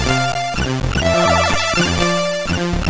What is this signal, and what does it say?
There is no code.